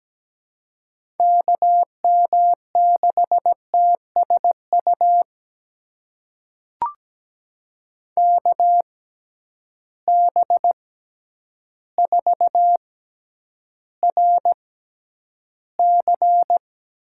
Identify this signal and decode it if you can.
Morse code, heard as a keyed tone, about 17 wpm, spaced as for 9 wpm, 700 Hz. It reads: KM6TSU KB4RC
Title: USA Call Signs - 6 Characters - 6x Character Spacing - 17wpm